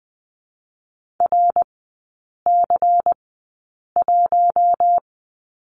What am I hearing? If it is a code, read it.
Morse code, heard as a keyed tone, about 20 wpm, spaced as for 8 wpm, 700 Hz. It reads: RC1
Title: 093 Lesson 16e - Letter c - Three Characters 20wpm